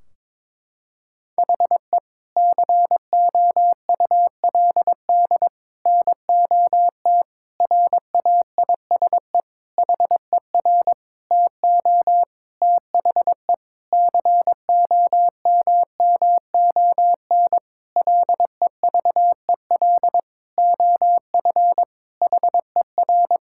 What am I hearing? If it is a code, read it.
Morse code, heard as a keyed tone, about 22 wpm, 700 Hz. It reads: HE COULD NOT RAISE HER TO THE COMMON LEVEL OF HER
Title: Sentences from Top 1600 Words 22wpm